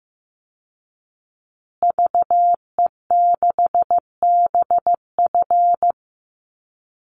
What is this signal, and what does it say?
Morse code, heard as a keyed tone, about 15 wpm, 700 Hz. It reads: VE6BF
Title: Call Signs Repeated 2x - 15wpm